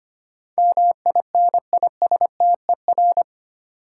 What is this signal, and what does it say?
Morse code, heard as a keyed tone, about 25 wpm, 700 Hz. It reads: MINISTER
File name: 8 Letter Words - First 3 Letters Given - 25wpm